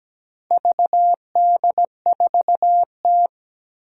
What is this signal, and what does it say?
Morse code, heard as a keyed tone, about 17 wpm, 700 Hz. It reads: VD4T
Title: Four Challenging Letter-Number Combinations 17wpm